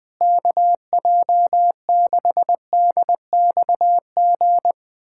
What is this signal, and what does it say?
Morse code, heard as a keyed tone, about 20 wpm, 700 Hz. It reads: KJ6DXG
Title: USA Callsign DE Callsign 20wpm